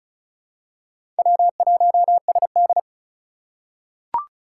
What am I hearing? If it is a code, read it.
Morse code, heard as a keyed tone, about 36 wpm, 700 Hz. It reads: W1SD
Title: USA Call Signs - 4 Characters - 2x Character Spacing - 35wpm